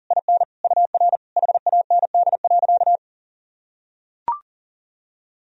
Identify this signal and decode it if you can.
Morse code, heard as a keyed tone, about 40 wpm, 700 Hz. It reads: IN UR HAND.
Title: Sentences from Top 400 Words and Abbreviations 40wpm